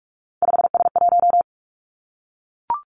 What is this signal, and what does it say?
Morse code, heard as a keyed tone, about 46 wpm, 700 Hz. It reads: 5S1
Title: Three Challenging Letter-Number Combinations 45wpm